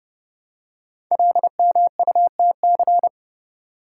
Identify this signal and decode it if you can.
Morse code, heard as a keyed tone, about 30 wpm, 700 Hz. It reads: LMUTC
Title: Code Groups - 5 Letters - 30wpm